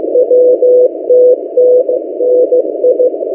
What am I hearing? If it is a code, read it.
Morse code, heard as a keyed tone, about 15 wpm, 500 Hz. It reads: WTTNI
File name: M08a- -1355utc-8096khz